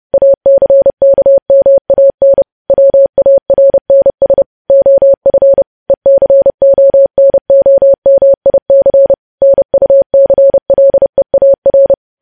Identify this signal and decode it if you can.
Morse code, heard as a keyed tone, about 30 wpm, 550 Hz. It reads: ACKMAN WARNS OF ECONOMIC NUCLEAR
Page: news headlines in morse code at 30 wpm